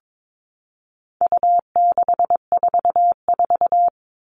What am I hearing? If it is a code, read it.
Morse code, heard as a keyed tone, about 22 wpm, 700 Hz. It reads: U644
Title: All Cut Numbers - 4-Digit 22wpm